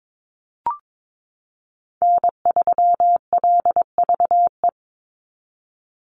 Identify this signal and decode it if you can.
Morse code, heard as a keyed tone, about 22 wpm, 700 Hz. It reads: N3LVE
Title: USA Call Signs 22wpm